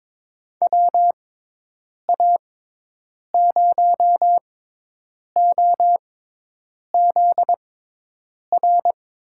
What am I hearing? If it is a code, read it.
Morse code, heard as a keyed tone, about 22 wpm, spaced as for 8 wpm, 700 Hz. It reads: WA0OZR
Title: USA Call Signs - 6 Characters - 6x Character Spacing - 22wpm